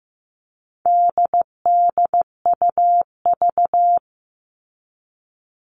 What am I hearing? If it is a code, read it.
Morse code, heard as a keyed tone, about 15 wpm, 700 Hz. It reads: DDUV
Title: All Cut Numbers - 4-Digit 15wpm